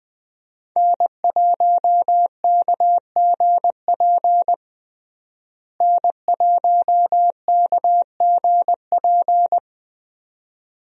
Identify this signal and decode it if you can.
Morse code, heard as a keyed tone, about 20 wpm, 700 Hz. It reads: N1KGP N1KGP
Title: USA Call Signs - 5 Characters - Repeated 3x - 20wpm